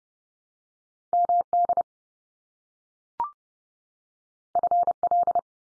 Morse code, heard as a keyed tone, about 32 wpm, 700 Hz. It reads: MD FL
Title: US States 30WPM